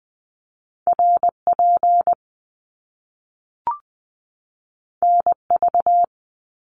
Morse code, heard as a keyed tone, about 20 wpm, 700 Hz. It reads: RP NV
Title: Top Two Letter Combinations in Words 20wpm